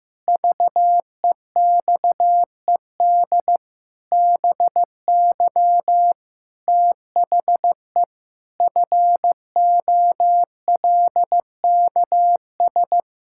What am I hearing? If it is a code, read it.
Morse code, heard as a keyed tone, about 15 wpm, 700 Hz. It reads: VEXED BY THE FOLKS
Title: Pangram Challenge v2 - 15wpm